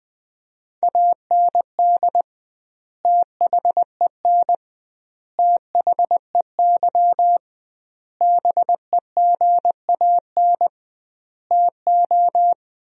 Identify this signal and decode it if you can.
Morse code, heard as a keyed tone, about 20 wpm, 700 Hz. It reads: AND THEN THEY BEGAN TO
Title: 2x Word-Spacing - Sentences from Top 300 Words - 20wpm